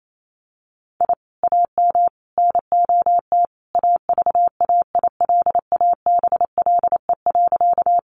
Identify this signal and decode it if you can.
Morse code, heard as a keyed tone, about 28 wpm, 700 Hz. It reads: I AM NOT AVAILABLE.
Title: Sentences from Top 600 Words 28wpm